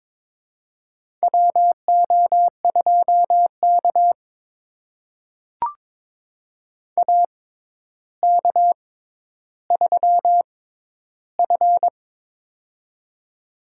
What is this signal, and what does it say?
Morse code, heard as a keyed tone, about 22 wpm, spaced as for 11 wpm, 700 Hz. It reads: WO2K AK3F
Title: USA Call Signs - 4 Characters - 6x Character Spacing - 22wpm